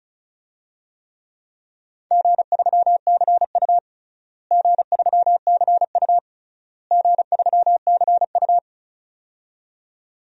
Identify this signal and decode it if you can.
Morse code, heard as a keyed tone, about 34 wpm, 700 Hz. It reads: G3CU G3CU G3CU
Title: DXCC-100 Call Signs - 4 Characters - Repeated 3x - 35wpm